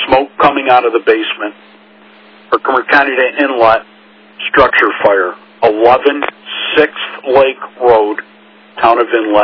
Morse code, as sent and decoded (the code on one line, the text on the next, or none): none